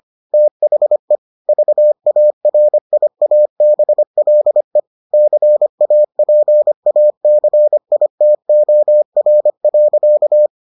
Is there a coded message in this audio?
Yes: Morse code from 0.3 to 10.5 s, heard as a keyed tone, about 25 wpm, 600 Hz. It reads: THE VARIABLE CAPACITOR.